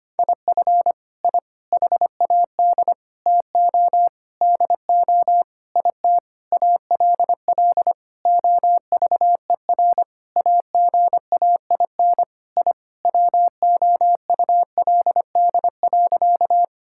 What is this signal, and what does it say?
Morse code, heard as a keyed tone, about 25 wpm, 700 Hz. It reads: IF I HAD TO DO IT ALL OVER AGAIN I WOULD.
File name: practice / 25wpm / Sentences with 9-Word Phrases - 25wpm